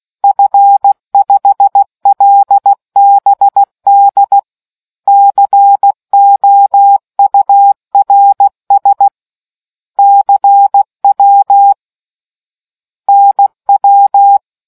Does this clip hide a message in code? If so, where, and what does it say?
Yes: Morse code from 0.2 to 14.4 s, heard as a keyed tone, about 16 wpm, 800 Hz. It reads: F5LBD COURS CW NW